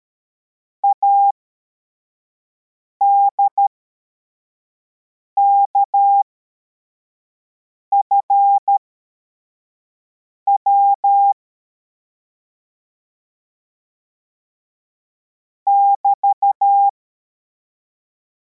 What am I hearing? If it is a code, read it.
Morse code, heard as a keyed tone, about 13 wpm, spaced as for 4 wpm, 800 Hz. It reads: ADKFW <BT>